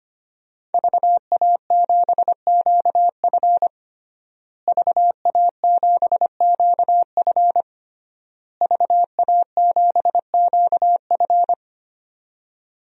Morse code, heard as a keyed tone, about 25 wpm, 700 Hz. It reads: VA7QF VA7QF VA7QF
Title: DXCC-100 Call Signs - 5 Characters - Repeated 3x - 25wpm